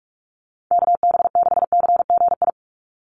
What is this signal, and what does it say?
Morse code, heard as a keyed tone, about 44 wpm, 700 Hz. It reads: KB6CGI
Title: USA Call Signs 45wpm